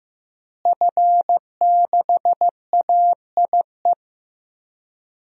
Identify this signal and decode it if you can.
Morse code, heard as a keyed tone, about 15 wpm, 700 Hz. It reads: F6AIE
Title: DXCC-100 Call Signs - 5 Characters - Repeated 3x - 15wpm